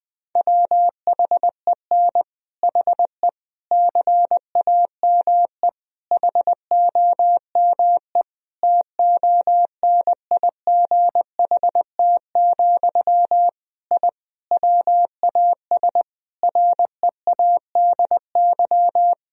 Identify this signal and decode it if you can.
Morse code, heard as a keyed tone, about 21 wpm, 700 Hz. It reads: WHEN HE CAME HOME TONIGHT, I WAS READY
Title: Sentences from Top 1000 Words - Review Part 3 - 20wpm